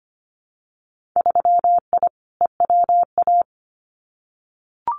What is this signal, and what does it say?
Morse code, heard as a keyed tone, about 25 wpm, 700 Hz. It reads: 3I EWA
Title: Winter Field Day Exchanges Encore 25wpm